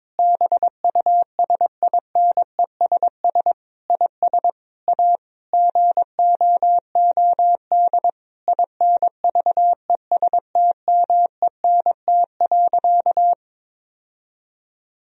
Morse code, heard as a keyed tone, about 21 wpm, 700 Hz. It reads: BUSINESS IS A GOOD INVESTMENT.
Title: Sentences from Top 1000 Words - Review Part 4 - 22wpm